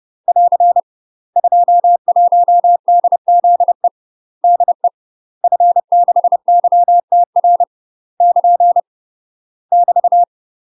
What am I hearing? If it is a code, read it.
Morse code, heard as a keyed tone, about 30 wpm, 700 Hz. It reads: <AR> 21DZE DE F6YTR <KN> <BT>
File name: UK MASTER QSOs TRAINING / 30WPM / UK MASTER QSOs 0045